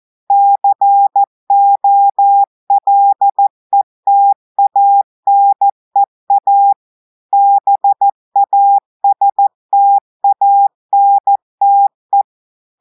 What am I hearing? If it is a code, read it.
Morse code, heard as a keyed tone, about 14 wpm, 800 Hz. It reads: COLETANEA BASTANTE
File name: audios / portugais / portugais texte en clair 1 vitesse 14 mots minutes